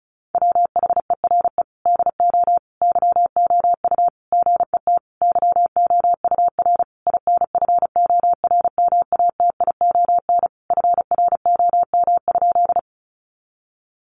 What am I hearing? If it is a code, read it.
Morse code, heard as a keyed tone, about 34 wpm, 700 Hz. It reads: WHERE DO YOU GET YOUR INFORMATION FROM?
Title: Sentences from Top 600 Words 35wpm